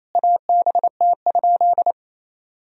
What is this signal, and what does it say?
Morse code, heard as a keyed tone, about 28 wpm, 700 Hz. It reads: ABT?
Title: Sentences from Top 100 Words and Abbreviations 28wpm